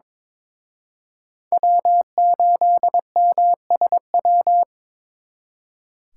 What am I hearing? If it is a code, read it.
Morse code, heard as a keyed tone, about 26 wpm, 700 Hz. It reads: W8MSW
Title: Speed Racing - USA Call Signs 22wpm